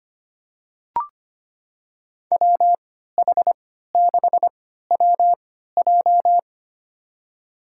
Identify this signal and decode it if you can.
Morse code, heard as a keyed tone, about 25 wpm, spaced as for 14 wpm, 700 Hz. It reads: WH6WJ
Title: USA Call Signs - 3x Character Spacing - 25wpm